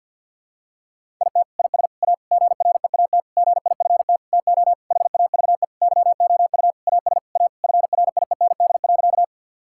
Morse code, heard as a keyed tone, about 52 wpm, 700 Hz. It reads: IT IS A GREAT GIFT TO HAVE YOU AS A FRIEND.
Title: Sentences from Top 1000 Words - Review Part 6 - 50wpm